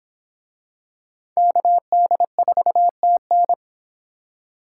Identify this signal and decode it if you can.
Morse code, heard as a keyed tone, about 23 wpm, 700 Hz. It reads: KD4TN